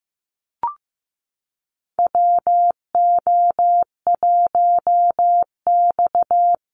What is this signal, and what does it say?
Morse code, heard as a keyed tone, about 15 wpm, 700 Hz. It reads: WO1X
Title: Call Signs Repeated 2x - 15wpm